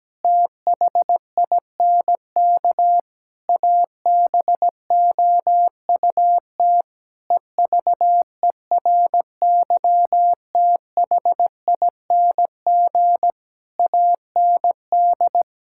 Morse code, heard as a keyed tone, about 17 wpm, 700 Hz. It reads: THINK ABOUT EVERYTHING AND
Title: Sentences from Top 1600 Words 17wpm